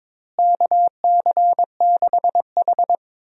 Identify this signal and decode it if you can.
Morse code, heard as a keyed tone, about 22 wpm, 700 Hz. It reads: KC6H
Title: USA Call Signs - 4 Characters - Repeated 2x - 22wpm